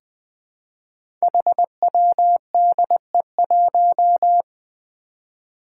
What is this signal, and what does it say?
Morse code, heard as a keyed tone, about 20 wpm, 700 Hz. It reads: HWDE1
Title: Five Challenging Letter-Number Combinations 20wpm